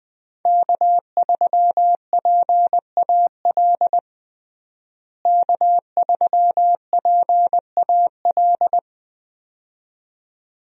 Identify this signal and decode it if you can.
Morse code, heard as a keyed tone, about 21 wpm, 700 Hz. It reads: K3PAL K3PAL
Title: USA Call Signs - 5 Characters - Repeated 2x - 20wpm